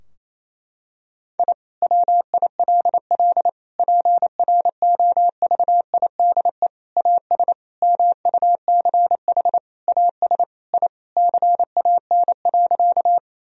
Morse code, heard as a keyed tone, about 28 wpm, 700 Hz. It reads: I WILL PROVIDE AS MUCH AS I CAN.
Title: Sentences from Top 1000 Words - Review Part 3 - 28wpm